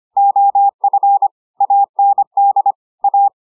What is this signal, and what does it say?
Morse code, heard as a keyed tone, about 25 wpm, 800 Hz. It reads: OF AND A